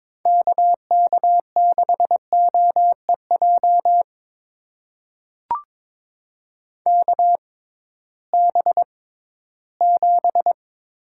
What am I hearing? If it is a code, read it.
Morse code, heard as a keyed tone, about 22 wpm, spaced as for 14 wpm, 700 Hz. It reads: KK6OEJ KB7
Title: USA Call Signs - 6 Characters - 6x Character Spacing - 22wpm